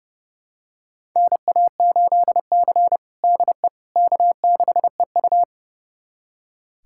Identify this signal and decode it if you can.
Morse code, heard as a keyed tone, about 30 wpm, 700 Hz. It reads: NA8C DE K6EU
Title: USA Callsign DE Callsign - 4 Characters - 30wpm